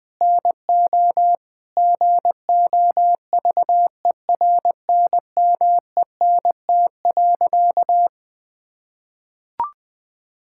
Morse code, heard as a keyed tone, about 20 wpm, 700 Hz. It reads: NO GOVERNMENT.
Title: Sentences from Top 1000 Words - Review Part 5 - 20wpm